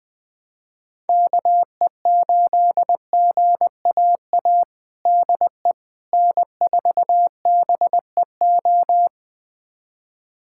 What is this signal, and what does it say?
Morse code, heard as a keyed tone, about 20 wpm, 700 Hz. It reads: KE8GAA DE N4BEO
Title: USA Callsign DE Callsign 20wpm